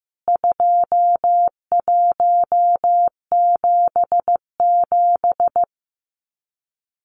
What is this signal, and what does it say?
Morse code, heard as a keyed tone, about 15 wpm, 700 Hz. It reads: 2177
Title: CWT Exchange 15wpm